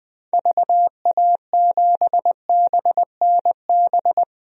Morse code, heard as a keyed tone, about 20 wpm, 700 Hz. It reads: VA7BNB
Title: DXCC-100 Call Signs - 6 Characters - 20wpm